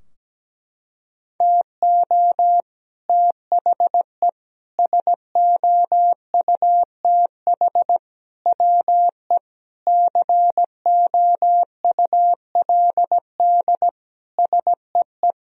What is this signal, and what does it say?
Morse code, heard as a keyed tone, about 17 wpm, 700 Hz. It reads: TO THE SOUTH WE COULD SEE
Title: Sentences from Top 1000 Words - Review Part 2 - 17wpm